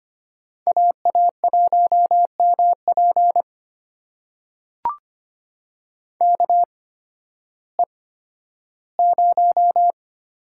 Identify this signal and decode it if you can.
Morse code, heard as a keyed tone, about 25 wpm, spaced as for 13 wpm, 700 Hz. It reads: AA1MP KE0
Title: USA Call Signs - 8x Character Spacing - 25wpm